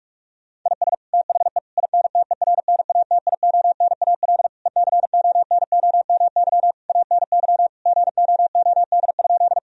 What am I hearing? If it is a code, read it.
Morse code, heard as a keyed tone, about 45 wpm, 700 Hz. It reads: IS THE INTERNATIONAL ECONOMY ANY GOOD?
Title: Sentences from Top 1000 Words - Review Part 4 - 45wpm